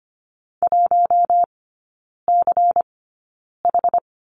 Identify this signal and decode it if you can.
Morse code, heard as a keyed tone, about 25 wpm, spaced as for 9 wpm, 700 Hz. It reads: 1CH
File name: course / 25:10wpm / 093 Lesson 16e - Letter c - Three Characters 25wpm